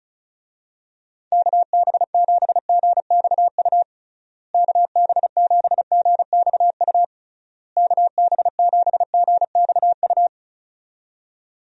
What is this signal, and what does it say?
Morse code, heard as a keyed tone, about 36 wpm, 700 Hz. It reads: KB7GXU KB7GXU KB7GXU
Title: USA Call Signs - 6 Characters - Repeated 3x - 35wpm